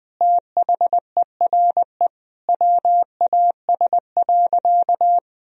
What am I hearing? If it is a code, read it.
Morse code, heard as a keyed tone, about 20 wpm, 700 Hz. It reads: THERE WAS.
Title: Sentences from Top 1000 Words - Review Part 3 - 20wpm